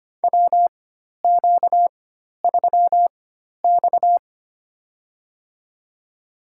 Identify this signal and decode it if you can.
Morse code, heard as a keyed tone, about 25 wpm, spaced as for 12 wpm, 700 Hz. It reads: WQ3X
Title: USA Call Signs - 4 Characters - 4x Character Spacing - 25wpm